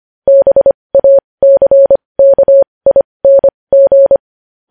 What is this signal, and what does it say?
Morse code, heard as a keyed tone, about 25 wpm, spaced as for 20 wpm, 550 Hz. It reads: BACKING
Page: news-headlines-in-morse-code-at-20-wpm